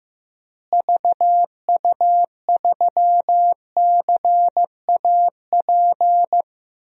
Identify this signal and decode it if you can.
Morse code, heard as a keyed tone, about 15 wpm, 700 Hz. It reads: VU3CAP
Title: DXCC-100 Call Signs - 6 Characters - Repeated 3x - 15wpm